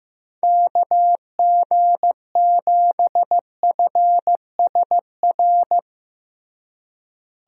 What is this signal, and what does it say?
Morse code, heard as a keyed tone, about 15 wpm, 700 Hz. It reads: KG7FSR
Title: USA Call Signs - 6 Characters - Repeated 2x - 15wpm